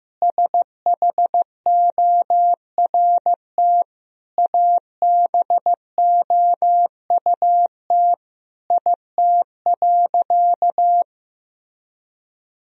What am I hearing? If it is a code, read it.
Morse code, heard as a keyed tone, about 15 wpm, 700 Hz. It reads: SHORT ABOUT IT.
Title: Sentences from Top 1000 Words - Review Part 2 - 15wpm